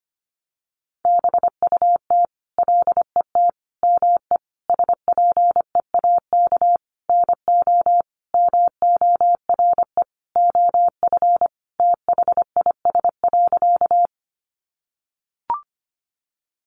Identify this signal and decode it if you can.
Morse code, heard as a keyed tone, about 25 wpm, 700 Hz. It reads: BUT LET ME SPEAK NO MORE OF THIS.
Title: Sentences from Top 1100 Words 25wpm